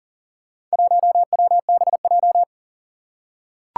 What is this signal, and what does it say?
Morse code, heard as a keyed tone, about 42 wpm, 700 Hz. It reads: 1WBJ
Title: Four Challenging Letter-Number Combinations 40wpm